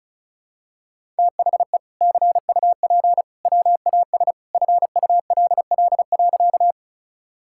Morse code, heard as a keyed tone, about 36 wpm, 700 Hz. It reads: THE CUP WAS FULL.
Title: Sentences from Top 1600 Words 35wpm